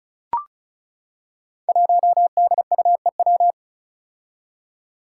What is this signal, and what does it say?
Morse code, heard as a keyed tone, about 36 wpm, 700 Hz. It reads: 1DUEW
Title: Five Challenging Letter-Number Combinations 35wpm